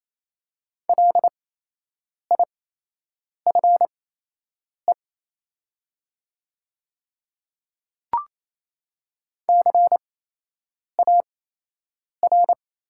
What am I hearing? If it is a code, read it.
Morse code, heard as a keyed tone, about 28 wpm, spaced as for 8 wpm, 700 Hz. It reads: LIFE CAR
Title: 8x Character Spacing - Top 300 Words - 28wpm